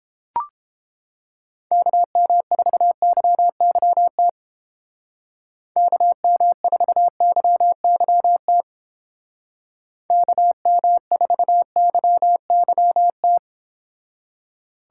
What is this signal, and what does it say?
Morse code, heard as a keyed tone, about 31 wpm, 700 Hz. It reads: KM4YYT KM4YYT KM4YYT